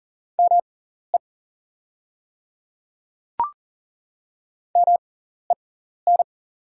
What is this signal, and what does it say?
Morse code, heard as a keyed tone, about 40 wpm, spaced as for 14 wpm, 700 Hz. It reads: ME MEN
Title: 6x Character Spacing - Top 500 Words - 40wpm